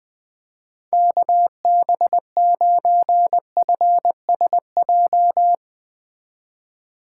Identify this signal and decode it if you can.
Morse code, heard as a keyed tone, about 20 wpm, 700 Hz. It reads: KB9FSJ
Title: Speed Racing - USA 2x3 Call Signs 17wpm